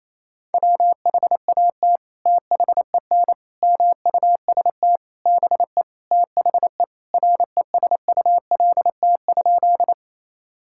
Morse code, heard as a keyed tone, about 28 wpm, 700 Hz. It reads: WHAT THEN MUST BE THE RESULT?
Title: Sentences from Top 1000 Words - Review Part 5 - 28wpm